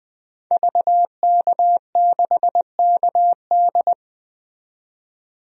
Morse code, heard as a keyed tone, about 21 wpm, 700 Hz. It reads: VK6KD
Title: DXCC-100 Call Signs - 5 Characters - 20wpm